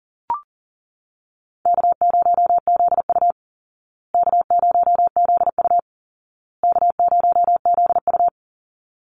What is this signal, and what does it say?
Morse code, heard as a keyed tone, about 40 wpm, 700 Hz. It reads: K0ZU K0ZU K0ZU
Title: Easy Call Signs Repeated 3x - 40wpm